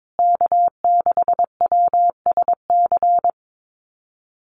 Morse code, heard as a keyed tone, about 22 wpm, 700 Hz. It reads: K6WSC
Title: USA Call Signs Repeated 2x 22wpm